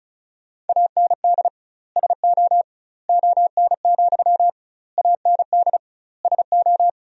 Morse code, heard as a keyed tone, about 36 wpm, 700 Hz. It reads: AND SO ON, AND SO
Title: 2x Word-Spacing - Sentences from Top 100 Words - 35wpm